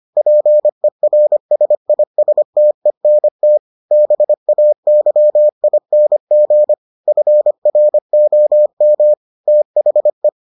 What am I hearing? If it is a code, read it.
Morse code, heard as a keyed tone, about 25 wpm, 600 Hz. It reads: PERSISTENT BAYING FROM THE